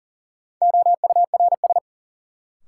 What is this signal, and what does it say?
Morse code, heard as a keyed tone, about 38 wpm, 700 Hz. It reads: OURS
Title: Sentences from Top 1000 Words - Review Part 5 - New Word Drill - 40wpm